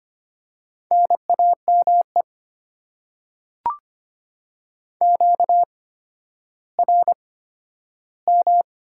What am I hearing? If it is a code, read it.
Morse code, heard as a keyed tone, about 25 wpm, spaced as for 12 wpm, 700 Hz. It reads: NAME QRM
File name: Top 90 QSO Elements - 8x Character Spacing - 25wpm